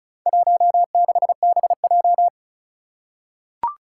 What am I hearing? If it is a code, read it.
Morse code, heard as a keyed tone, about 36 wpm, 700 Hz. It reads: 16BJ
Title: Four Challenging Letter-Number Combinations 35wpm